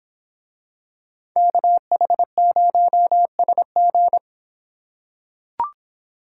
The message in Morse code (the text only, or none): KH0SG